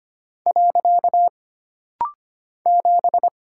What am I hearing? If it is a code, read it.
Morse code, heard as a keyed tone, about 25 wpm, 700 Hz. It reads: . 7
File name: Single Character (Letter:Number:Punctuation:Prosign) - Rapid-Fire - 25wpm